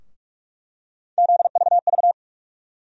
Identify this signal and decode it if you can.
Morse code, heard as a keyed tone, about 42 wpm, 700 Hz. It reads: ZVV
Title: USA Call Sign Postfixes - 3 Letters 45wpm